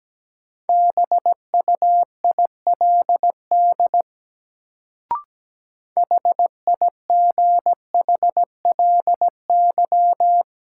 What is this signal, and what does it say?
Morse code, heard as a keyed tone, about 17 wpm, 700 Hz. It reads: BUILD HIGHLY
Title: Sentences from Top 1100 Words - New Word Drill Alt - 17wpm